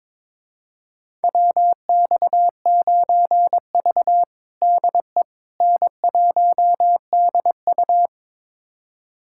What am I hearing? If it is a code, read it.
Morse code, heard as a keyed tone, about 22 wpm, 700 Hz. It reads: WX9V DE N1DU
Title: USA Callsign DE Callsign - 4 Characters - 22wpm